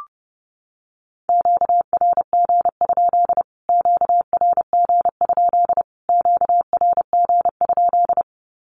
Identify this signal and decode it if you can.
Morse code, heard as a keyed tone, about 30 wpm, 700 Hz. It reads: QRG? QRG? QRG?
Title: Q-Codes Repeated 3x 30wpm